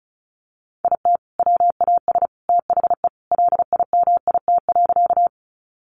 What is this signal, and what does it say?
Morse code, heard as a keyed tone, about 34 wpm, 700 Hz. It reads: IT WAS THE LIMIT.
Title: Sentences from Top 1500 Words 35wpm